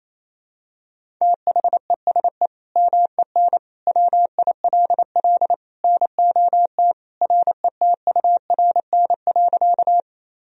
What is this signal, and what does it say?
Morse code, heard as a keyed tone, about 28 wpm, 700 Hz. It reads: THESE MEN WILL NOT RETURN.
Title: Sentences from Top 1000 Words - Review Part 1 - 28wpm